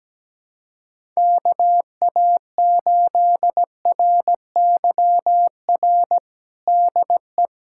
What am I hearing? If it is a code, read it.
Morse code, heard as a keyed tone, about 17 wpm, 700 Hz. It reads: KA8RYR DE
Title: USA Callsign DE Callsign 17wpm